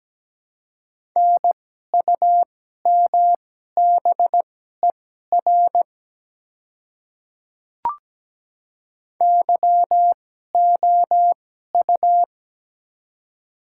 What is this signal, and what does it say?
Morse code, heard as a keyed tone, about 17 wpm, spaced as for 12 wpm, 700 Hz. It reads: NUMBER YOU
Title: 2x Character Spacing - Top 100 Words - 17wpm